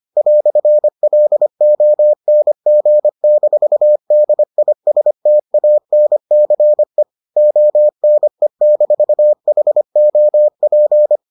Morse code, heard as a keyed tone, about 25 wpm, 600 Hz. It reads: "LONG-DISTANCE ONE-HOP